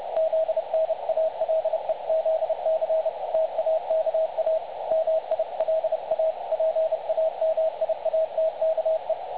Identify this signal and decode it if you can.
Morse code, heard as a keyed tone, about 30 wpm, 660 Hz. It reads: ZNALEZC TAKA MILA PAMIATKE
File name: dlazaawansowanych / nagraniazpasma / websdr recording 3565.0kHz